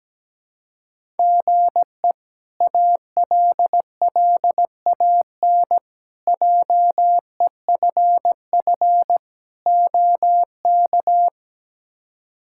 Morse code, heard as a keyed tone, about 17 wpm, 700 Hz. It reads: GE ALLAN JEFF OK
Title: K1USN SST QSO Practice - 17wpm